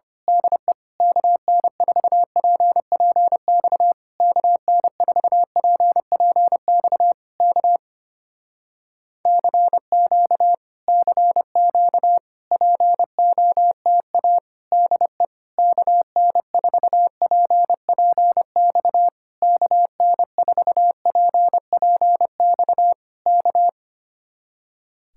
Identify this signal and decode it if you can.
Morse code, heard as a keyed tone, about 26 wpm, 700 Hz. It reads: DE KN4PPX KN4PPX K CQ CQ POTA DE KN4PPX KN4PPX K
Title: Speed-Racing Common QSO Phrases - 25wpm